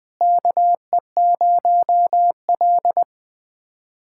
Morse code, heard as a keyed tone, about 21 wpm, 700 Hz. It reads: KE0L
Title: K1USN SST QSO Practice - 20wpm